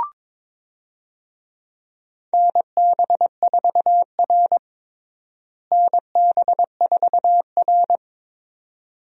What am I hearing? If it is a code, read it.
Morse code, heard as a keyed tone, about 22 wpm, 700 Hz. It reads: NB4R NB4R
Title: USA Call Signs - 4 Characters - Repeated 2x - 22wpm